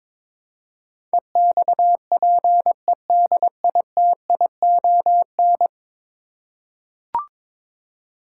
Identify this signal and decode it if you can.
Morse code, heard as a keyed tone, about 22 wpm, 700 Hz. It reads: EXPEDITION
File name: practice / 22wpm / 10 Letter Words - First 3 Letters Given - 22wpm